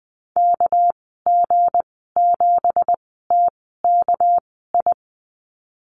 Morse code, heard as a keyed tone, about 19 wpm, spaced as for 14 wpm, 700 Hz. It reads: KG7TKI
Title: 219 Lesson 35g - Number 0 - Call Signs 20wpm